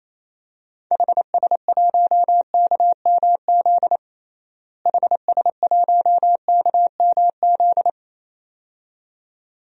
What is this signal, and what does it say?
Morse code, heard as a keyed tone, about 28 wpm, 700 Hz. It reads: HS1KMZ HS1KMZ
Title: DXCC-100 Call Signs - 6 Characters - Repeated 2x - 28wpm